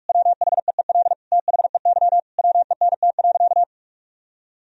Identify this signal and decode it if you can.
Morse code, heard as a keyed tone, about 46 wpm, 700 Hz. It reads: WHEEL THEY WENT.
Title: Sentences from Top 1000 Words - Review Part 3 - 45wpm